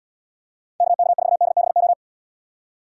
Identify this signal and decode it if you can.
Morse code, heard as a keyed tone, about 78 wpm, 700 Hz. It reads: KK4DBO